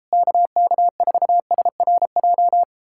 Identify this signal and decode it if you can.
Morse code, heard as a keyed tone, about 38 wpm, 700 Hz. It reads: KK4SRJ